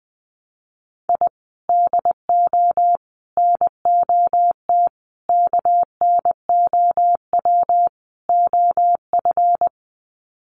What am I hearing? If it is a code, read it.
Morse code, heard as a keyed tone, about 20 wpm, 700 Hz. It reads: I DO NOT KNOW OF
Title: Sets of 5 Words Encore2 20wpm